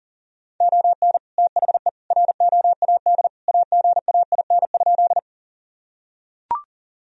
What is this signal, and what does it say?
Morse code, heard as a keyed tone, about 40 wpm, 700 Hz. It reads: ON THE ROAD AGAIN?
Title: Sentences from Top 1000 Words - Review Part 2 - 40wpm